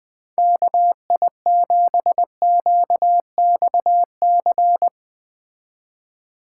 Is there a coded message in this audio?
Yes: Morse code from 0.4 to 4.9 s, heard as a keyed tone, about 20 wpm, 700 Hz. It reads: KI7QXC